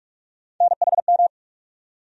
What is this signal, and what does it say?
Morse code, heard as a keyed tone, about 43 wpm, 700 Hz. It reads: NHM
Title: USA Call Sign Postfixes - 3 Letters 45wpm